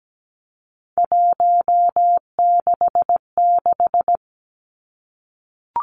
Morse code, heard as a keyed tone, about 17 wpm, 700 Hz. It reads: 166
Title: Three Challenging Letter-Number Combinations 17wpm